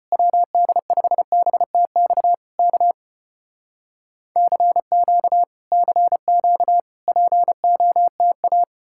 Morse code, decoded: WD5BTX K CQ CQ POTA

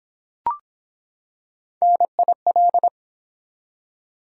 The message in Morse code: NIL

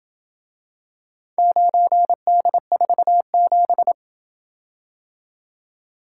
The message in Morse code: 9D47

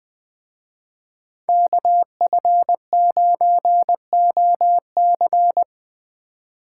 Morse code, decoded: KF9OC

